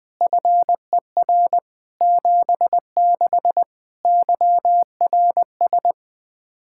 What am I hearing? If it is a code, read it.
Morse code, heard as a keyed tone, about 20 wpm, 700 Hz. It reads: FER 76 YRS
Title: Practice QSO - Exchange 3 - More Info - 20wpm